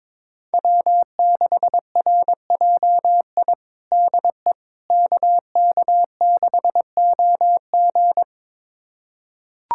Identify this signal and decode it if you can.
Morse code, heard as a keyed tone, about 22 wpm, 700 Hz. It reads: W6RJI DE KK6OG
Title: USA Callsign DE Callsign - 5 Characters - 22wpm